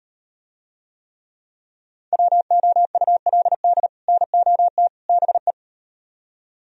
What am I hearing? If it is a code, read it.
Morse code, heard as a keyed tone, about 38 wpm, 700 Hz. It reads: WOULD NOT BE